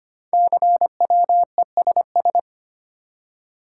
Morse code, heard as a keyed tone, about 25 wpm, 700 Hz. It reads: CWESS
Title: Code Groups - 5 Letters - 25wpm